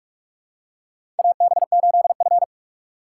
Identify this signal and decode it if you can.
Morse code, heard as a keyed tone, about 42 wpm, 700 Hz. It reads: AB8F
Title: USA Call Signs - 4 Characters - 4x Character Spacing - 45wpm